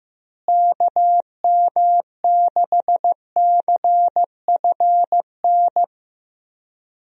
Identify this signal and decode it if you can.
Morse code, heard as a keyed tone, about 15 wpm, 700 Hz. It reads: KM6CFN
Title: USA Call Signs - 6 Characters - Repeated 2x - 15wpm